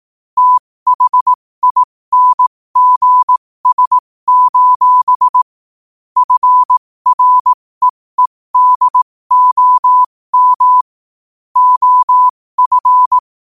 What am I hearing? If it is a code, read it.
Morse code, heard as a keyed tone, about 18 wpm, spaced as for 15 wpm, 1000 Hz. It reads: THINGS: FREEDOM OF
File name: qotdfiles / 15wpm